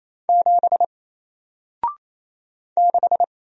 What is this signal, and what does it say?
Morse code, heard as a keyed tone, about 28 wpm, 700 Hz. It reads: 7 6
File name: Single-Letter-Number-28wpm